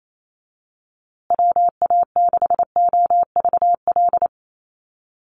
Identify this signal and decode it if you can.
Morse code, heard as a keyed tone, about 28 wpm, 700 Hz. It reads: WA6OVL
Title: USA Call Signs Repeated 2x 28wpm